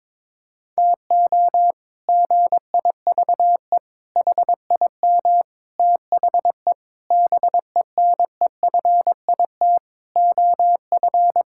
Morse code, heard as a keyed tone, about 22 wpm, 700 Hz. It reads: TO GIVE HIM THE BENEFIT OF
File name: practice / 22wpm / Sentences with 8-Word Phrases - 22wpm